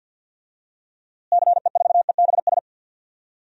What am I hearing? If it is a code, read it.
Morse code, heard as a keyed tone, about 52 wpm, 700 Hz. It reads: CE4EBS
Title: DXCC-100 Call Signs - 6 Characters - Repeated 2x - 50wpm